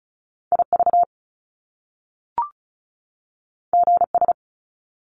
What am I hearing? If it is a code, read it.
Morse code, heard as a keyed tone, about 36 wpm, 700 Hz. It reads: IV GS
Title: Top Two Letter Combinations in Words 35wpm